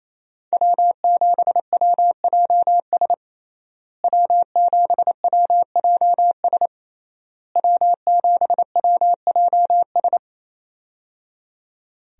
Morse code, heard as a keyed tone, about 28 wpm, 700 Hz. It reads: W7WJS W7WJS W7WJS
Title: USA Call Signs - 5 Characters - Repeated 3x - 28wpm